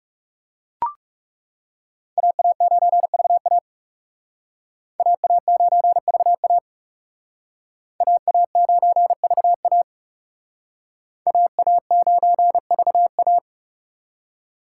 Morse code, heard as a keyed tone, about 38 wpm, 700 Hz. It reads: AA9VA AA9VA AA9VA AA9VA